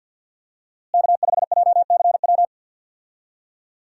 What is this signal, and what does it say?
Morse code, heard as a keyed tone, about 48 wpm, 690 Hz. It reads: K5JCW
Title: USA Call Signs - 5 Characters - Repeated 3x - 50wpm